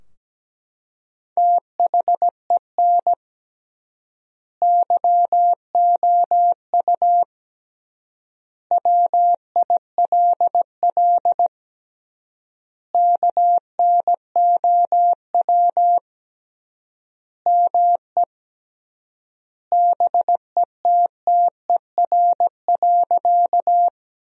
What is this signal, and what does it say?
Morse code, heard as a keyed tone, about 17 wpm, 700 Hz. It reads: THEN YOU WILL KNOW ME BETTER.
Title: 3x Word-Spacing - Sentences from Top 400 Words - 17wpm